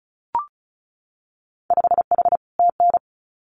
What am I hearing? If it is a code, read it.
Morse code, heard as a keyed tone, about 36 wpm, 700 Hz. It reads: 5H TN
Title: Winter Field Day Exchanges Encore 35wpm